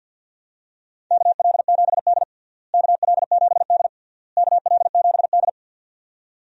Morse code, heard as a keyed tone, about 50 wpm, 700 Hz. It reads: KL7D KL7D KL7D
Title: DXCC-100 Call Signs - 4 Characters - Repeated 3x - 50wpm